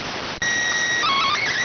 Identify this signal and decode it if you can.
SSTV (slow-scan television), PD 290.